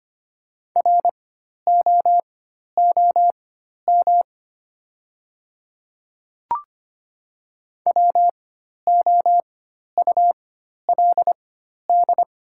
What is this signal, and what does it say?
Morse code, heard as a keyed tone, about 25 wpm, spaced as for 12 wpm, 700 Hz. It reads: ROOM WOULD
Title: 4x Character Spacing - Top 300 Words - 25wpm